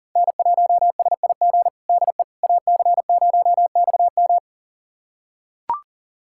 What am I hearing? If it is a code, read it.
Morse code, heard as a keyed tone, about 40 wpm, 700 Hz. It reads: N1SIG DE AC0XM
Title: USA Callsign DE Callsign - 5 Characters - 40wpm